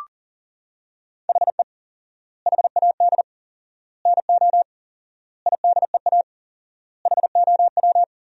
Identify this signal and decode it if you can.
Morse code, heard as a keyed tone, about 40 wpm, 700 Hz. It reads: HE HAD NO IDEA HOW
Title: 4x Word-Spacing - Sentences from Top 300 Words - 40wpm